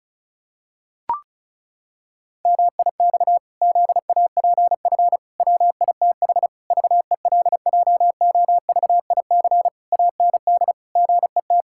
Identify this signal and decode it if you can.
Morse code, heard as a keyed tone, about 34 wpm, 700 Hz. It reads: MIX ZAPF WITH VELJOVIC AND GET
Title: Pangram Challenge v2 - 35wpm